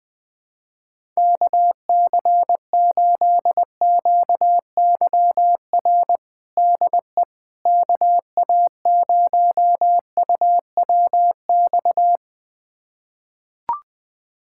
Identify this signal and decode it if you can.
Morse code, heard as a keyed tone, about 20 wpm, 700 Hz. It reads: KC8QYR DE KA0UWX
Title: USA Callsign DE Callsign - 6 Characters - 20wpm